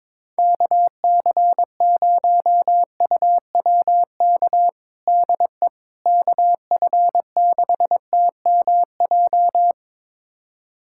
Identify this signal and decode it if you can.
Morse code, heard as a keyed tone, about 22 wpm, 700 Hz. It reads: KC0UWK DE KF6TMJ
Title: USA Callsign DE Callsign - 6 Characters - 22wpm